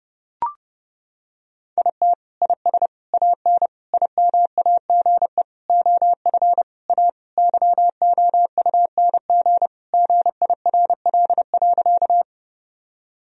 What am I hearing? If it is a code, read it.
Morse code, heard as a keyed tone, about 28 wpm, 700 Hz. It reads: IT IS AN IMAGE OF A YOUNG GIRL.
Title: Sentences from Top 1000 Words - Review Part 4 - 30wpm